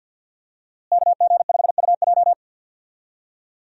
Morse code, heard as a keyed tone, about 45 wpm, 700 Hz. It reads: KG5UJ
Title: USA Call Signs - 5 Characters - Repeated 3x - 50wpm